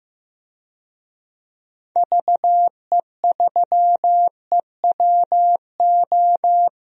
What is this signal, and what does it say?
Morse code, heard as a keyed tone, about 15 wpm, 700 Hz. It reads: VE3EWO
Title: DXCC-100 Call Signs - 6 Characters - Repeated 3x - 15wpm